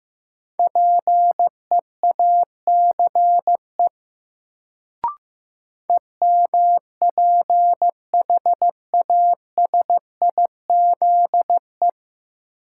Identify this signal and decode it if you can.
Morse code, heard as a keyed tone, about 15 wpm, 700 Hz. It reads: PEACE EMPHASIZE
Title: Sentences from Top 1500 Words - New Word Drill Alt - 15wpm